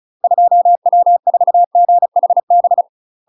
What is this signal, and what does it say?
Morse code, heard as a keyed tone, about 36 wpm, 700 Hz. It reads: 2W4GHB
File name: UK MASTER QSOs TRAINING / 35WPM / UK MASTER QSOs 0033